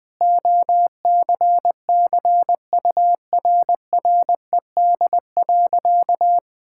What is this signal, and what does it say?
Morse code, heard as a keyed tone, about 20 wpm, 700 Hz. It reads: OCCURRED.
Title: Sentences with 6-Word Phrases - 20wpm